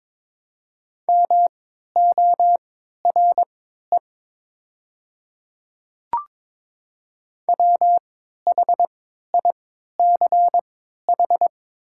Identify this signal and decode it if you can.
Morse code, heard as a keyed tone, about 22 wpm, spaced as for 13 wpm, 700 Hz. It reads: MORE WHICH
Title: 3x Character Spacing - Top 100 Words - 22wpm